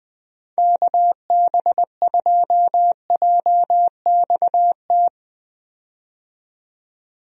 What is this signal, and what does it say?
Morse code, heard as a keyed tone, about 20 wpm, 700 Hz. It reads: KB2JXT
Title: USA Call Signs - 6 Characters - 20wpm